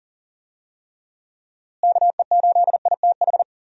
Morse code, heard as a keyed tone, about 40 wpm, 700 Hz. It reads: KE8ITH
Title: USA Call Signs - 6 Characters - Repeated 2x - 40wpm